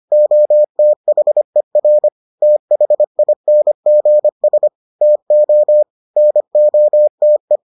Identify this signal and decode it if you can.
Morse code, heard as a keyed tone, about 25 wpm, 600 Hz. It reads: OTHER THINGS TO NOTE